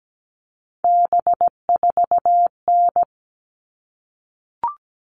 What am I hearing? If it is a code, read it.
Morse code, heard as a keyed tone, about 17 wpm, 700 Hz. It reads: B4N
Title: All Cut Numbers - 3-Digit 17wpm